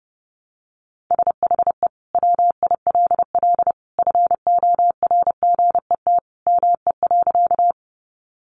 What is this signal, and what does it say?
Morse code, heard as a keyed tone, about 30 wpm, 700 Hz. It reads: SHE WILL FORGET ME.
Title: Sentences from Top 1200 Words 30wpm